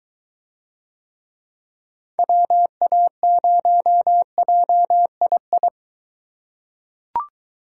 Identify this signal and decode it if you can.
Morse code, heard as a keyed tone, about 22 wpm, 700 Hz. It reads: WA0JII